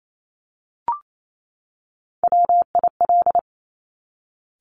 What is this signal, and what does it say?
Morse code, heard as a keyed tone, about 28 wpm, 700 Hz. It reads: WIL
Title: Top Three Letter Combinations in Words 28wpm